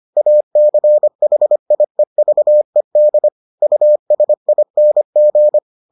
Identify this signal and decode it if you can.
Morse code, heard as a keyed tone, about 25 wpm, 600 Hz. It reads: ACHIEVED USING